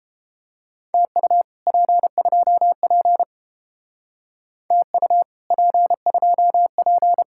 Morse code, heard as a keyed tone, about 31 wpm, 700 Hz. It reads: TU P2P TU P2P